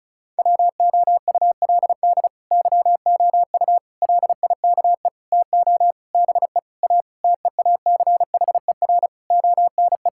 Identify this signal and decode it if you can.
Morse code, heard as a keyed tone, about 36 wpm, 700 Hz. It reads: WOULD YOU LIKE TO BE A TEACHER ONE
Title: Sentences from Top 1000 Words - Review Part 6 - 35wpm